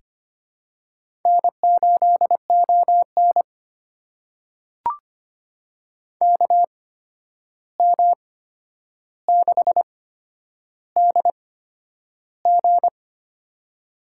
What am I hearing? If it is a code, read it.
Morse code, heard as a keyed tone, about 25 wpm, spaced as for 10 wpm, 700 Hz. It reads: N8ON KM6DG
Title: USA Call Signs - 8x Character Spacing - 25wpm